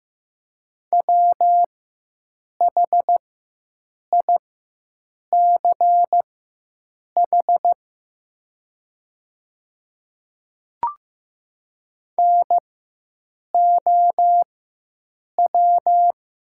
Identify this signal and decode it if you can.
Morse code, heard as a keyed tone, about 15 wpm, spaced as for 7 wpm, 690 Hz. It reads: WHICH NOW
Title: 4x Character Spacing - Top 500 Words - 15wpm